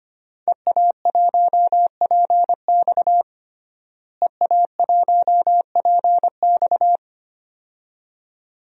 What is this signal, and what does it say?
Morse code, heard as a keyed tone, about 25 wpm, 700 Hz. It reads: EA1PX EA1PX
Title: DXCC-100 Call Signs - 5 Characters - Repeated 2x - 25wpm